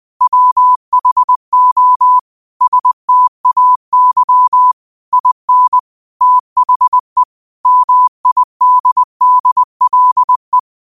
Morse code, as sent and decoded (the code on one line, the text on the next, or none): .-- .... --- / ... - .- -.-- / .. -. / - .... / -- .. -.. -.. .-.. .
WHO STAY IN THE MIDDLE